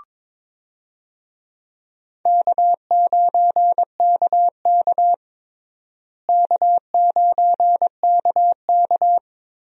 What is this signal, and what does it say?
Morse code, heard as a keyed tone, about 22 wpm, 700 Hz. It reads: K9KK K9KK
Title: USA Call Signs - 4 Characters - Repeated 3x - 22wpm